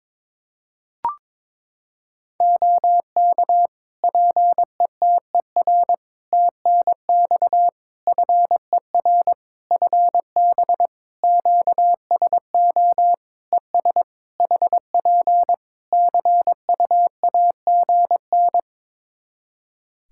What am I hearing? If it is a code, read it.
Morse code, heard as a keyed tone, about 22 wpm, 700 Hz. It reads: OK PETER TNX FER FB QSO ES HP CUAGN
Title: Practice QSO - Exchange 4 - Ending - 22wpm